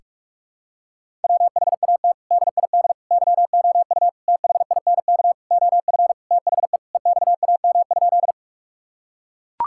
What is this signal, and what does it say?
Morse code, heard as a keyed tone, about 42 wpm, 700 Hz. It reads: WHAT DID YOU THINK OF THE EXAM?
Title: Sentences from Top 1000 Words - Review Part 5 - 45wpm